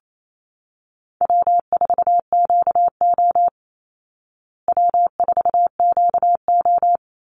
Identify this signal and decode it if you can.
Morse code, heard as a keyed tone, about 28 wpm, 700 Hz. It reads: W4QO W4QO
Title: Call Signs Repeated 3x - 28wpm